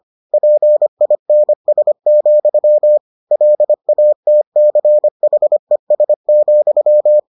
Morse code, heard as a keyed tone, about 25 wpm, 600 Hz. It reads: PINS, LATCHES,